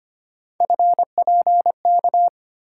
Morse code, heard as a keyed tone, about 25 wpm, 700 Hz. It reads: FPK
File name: Three Letter-Number Combinations - 25wpm